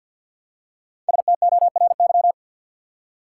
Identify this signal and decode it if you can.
Morse code, heard as a keyed tone, about 51 wpm, 700 Hz. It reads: STORY
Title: Five Letter Words - 50wpm